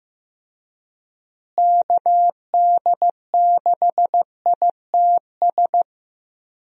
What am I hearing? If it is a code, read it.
Morse code, heard as a keyed tone, about 15 wpm, 700 Hz. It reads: KD6ITS
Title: USA Call Signs - 6 Characters - Repeated 2x - 15wpm